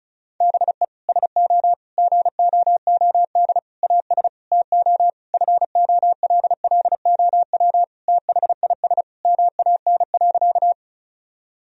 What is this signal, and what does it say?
Morse code, heard as a keyed tone, about 36 wpm, 700 Hz. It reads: BE SO GOOD AS TO FOLLOW THIS MAN.
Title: Sentences from Top 1000 Words - Review Part 1 - 35wpm